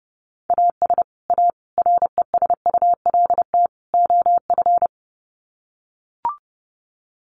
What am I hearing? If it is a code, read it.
Morse code, heard as a keyed tone, about 30 wpm, 700 Hz. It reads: AS A RESULT OF
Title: Sets of 4 Words 30wpm